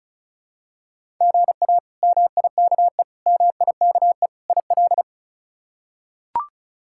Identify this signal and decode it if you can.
Morse code, heard as a keyed tone, about 34 wpm, 700 Hz. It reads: GA MIKE MIKE IL
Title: K1USN SST QSO Practice - 35wpm